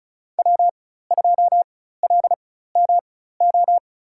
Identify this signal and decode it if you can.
Morse code, heard as a keyed tone, about 34 wpm, spaced as for 16 wpm, 700 Hz. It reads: W2LMO
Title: USA Call Signs - 5 Characters - 4x Character Spacing - 35wpm